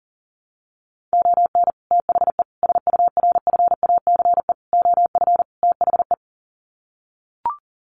Morse code, heard as a keyed tone, about 40 wpm, 700 Hz. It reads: ON THE SURFACE OF THE
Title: Sets of 5 Words Encore 40wpm